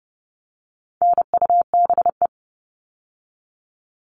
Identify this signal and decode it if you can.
Morse code, heard as a keyed tone, about 30 wpm, 700 Hz. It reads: NUBE
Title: All Cut Numbers - 4-Digit 30wpm